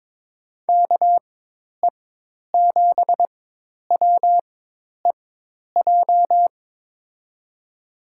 Morse code, heard as a keyed tone, about 21 wpm, spaced as for 10 wpm, 700 Hz. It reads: KE7WEJ
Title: USA Call Signs - 6 Characters - 4x Character Spacing - 22wpm